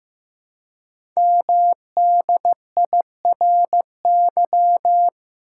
Morse code, heard as a keyed tone, about 15 wpm, 700 Hz. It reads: MDIRY
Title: Code Groups - 5 Letters - 15wpm